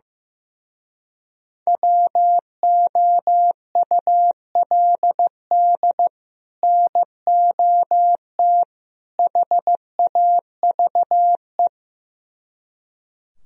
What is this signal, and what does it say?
Morse code, heard as a keyed tone, about 15 wpm, 700 Hz. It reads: WOULD NOT HAVE
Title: Speed-Racing - Sets of 3 Words 15wpm